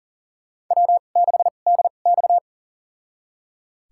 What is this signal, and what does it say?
Morse code, heard as a keyed tone, about 38 wpm, spaced as for 28 wpm, 700 Hz. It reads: W6DX
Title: USA Call Signs - 4 Characters - 2x Character Spacing - 40wpm